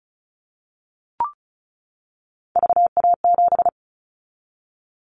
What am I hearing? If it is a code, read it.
Morse code, heard as a keyed tone, about 36 wpm, 700 Hz. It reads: VA7
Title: Callsign Prefixes for Top 100 DXCC - 35wpm